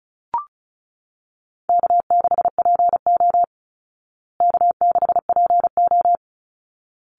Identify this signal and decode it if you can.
Morse code, heard as a keyed tone, about 34 wpm, 700 Hz. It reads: K6PO K6PO
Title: Easy Call Signs Repeated 2x - 35wpm